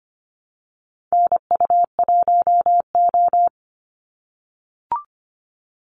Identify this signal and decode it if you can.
Morse code, heard as a keyed tone, about 25 wpm, 700 Hz. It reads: NU1O
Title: Call Signs - 25wpm